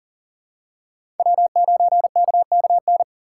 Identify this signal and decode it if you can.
Morse code, heard as a keyed tone, about 40 wpm, 700 Hz. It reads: W9KKN
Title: USA Call Signs - 5 Characters - 4x Character Spacing - 40wpm